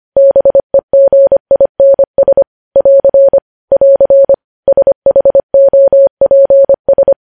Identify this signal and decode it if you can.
Morse code, heard as a keyed tone, about 25 wpm, 550 Hz. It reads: BEGINS <AR> <AR> SHOPS
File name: podcast / News headlines-1x-25x25